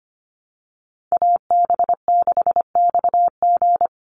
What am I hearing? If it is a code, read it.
Morse code, heard as a keyed tone, about 25 wpm, 700 Hz. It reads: AB6XG
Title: Call Signs - 25wpm